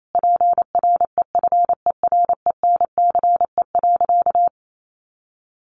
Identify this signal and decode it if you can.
Morse code, heard as a keyed tone, about 28 wpm, 700 Hz. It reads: PREFERENCE.
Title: Sentences from Top 1500 Words 28wpm